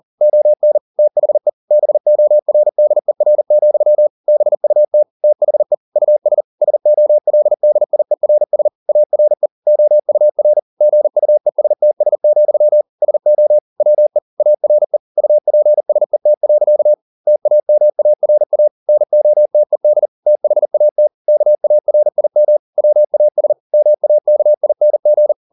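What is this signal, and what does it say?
Morse code, heard as a keyed tone, about 40 wpm, 600 Hz. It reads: ON THE BORDER, BUT THE US SOLDIERS ARE OUR GUESTS, SO WE ARE UPSET. TAMARA NOTED THAT KARIM WAS MAKING
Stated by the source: Morse code at 40 wpm, practice text, CW med Gnister.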